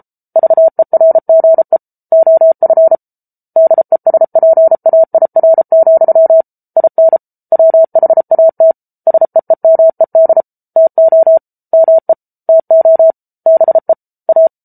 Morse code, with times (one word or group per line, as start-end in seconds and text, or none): 0.4-1.8 VERGE
2.1-3.0 OF
3.6-6.4 DESPAIR,
6.8-7.2 IN
7.5-8.7 WHAT
9.1-10.4 SEEMED
10.8-11.4 TO
11.7-12.1 ME
12.5-13.1 TO
13.5-13.9 BE
14.3-14.5 A